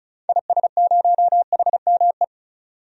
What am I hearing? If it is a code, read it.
Morse code, heard as a keyed tone, about 36 wpm, 700 Hz. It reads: IS0HME
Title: DXCC-100 Call Signs - 6 Characters - 35wpm